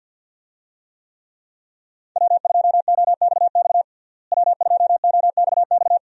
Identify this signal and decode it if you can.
Morse code, heard as a keyed tone, about 51 wpm, 700 Hz. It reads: W2OXX W2OXX
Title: USA Call Signs - 5 Characters - Repeated 2x - 50wpm